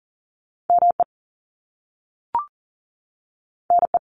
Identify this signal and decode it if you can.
Morse code, heard as a keyed tone, about 41 wpm, 700 Hz. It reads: ME NE
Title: US States 40wpm